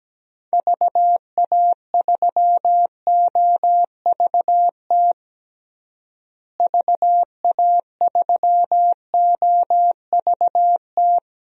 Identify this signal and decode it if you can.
Morse code, heard as a keyed tone, about 17 wpm, 700 Hz. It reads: VA3OVT VA3OVT
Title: DXCC-100 Call Signs - 6 Characters - Repeated 3x - 17wpm